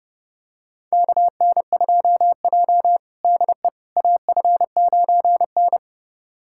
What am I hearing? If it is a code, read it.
Morse code, heard as a keyed tone, about 30 wpm, 700 Hz. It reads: KN2J DE AF9N
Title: USA Callsign DE Callsign - 4 Characters - 30wpm